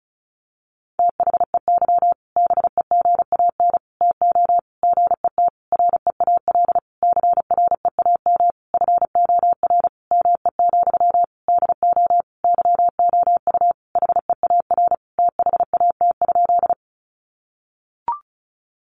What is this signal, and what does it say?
Morse code, heard as a keyed tone, about 34 wpm, 700 Hz. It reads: THEY BEGAN TO GET REAL CREAM FOR ME, DO YOU HEAR THAT?
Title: Sentences from Top 1500 Words 35wpm